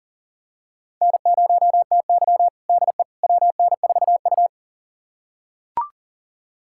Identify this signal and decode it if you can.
Morse code, heard as a keyed tone, about 40 wpm, 700 Hz. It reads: N0TY DE WN4U
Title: USA Callsign DE Callsign - 4 Characters - 40wpm